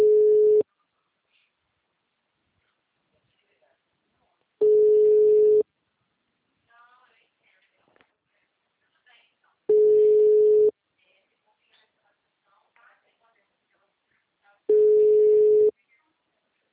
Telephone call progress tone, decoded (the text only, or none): ringback tone